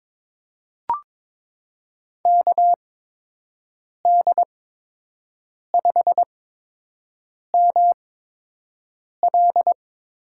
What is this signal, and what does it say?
Morse code, heard as a keyed tone, about 22 wpm, spaced as for 6 wpm, 700 Hz. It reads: KD5ML
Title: USA Call Signs - 6 Characters - 8x Character Spacing - 22wpm